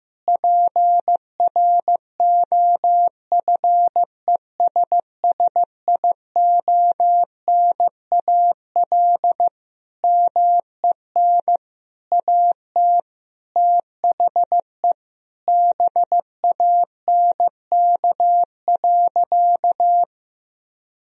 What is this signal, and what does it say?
Morse code, heard as a keyed tone, about 15 wpm, 700 Hz. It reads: PROFESSIONAL MEN AT THE BANK.
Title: Sentences from Top 1000 Words - Review Part 4 - 15wpm